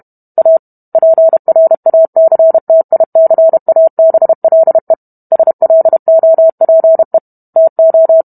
Morse code, heard as a keyed tone, about 32 wpm, 650 Hz. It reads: A PRACTICABLE SLOPE TO